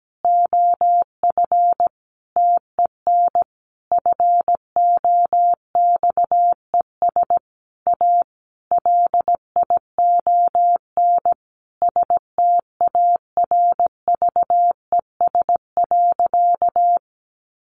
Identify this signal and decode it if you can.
Morse code, heard as a keyed tone, about 17 wpm, 700 Hz. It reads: OF TEN FOXES A LION STARVES.
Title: A Sound Of Thunder 17wpm